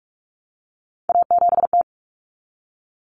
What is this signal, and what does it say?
Morse code, heard as a keyed tone, about 44 wpm, 700 Hz. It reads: A7T
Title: Common Cut Numbers - 3-Digit 45wpm